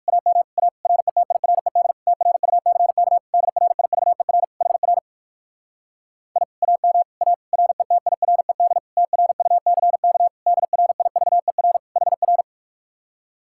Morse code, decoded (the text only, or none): AM A RETIRED TRUCK DRIVER HR I AM A RETIRED TRUCK DRIVER HR